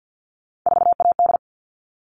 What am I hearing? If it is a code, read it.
Morse code, heard as a keyed tone, about 50 wpm, 700 Hz. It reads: 4AD
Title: All Cut Numbers - 3-Digit 50wpm